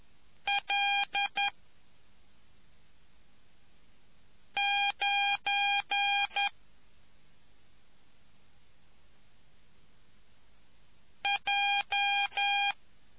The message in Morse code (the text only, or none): L 9 J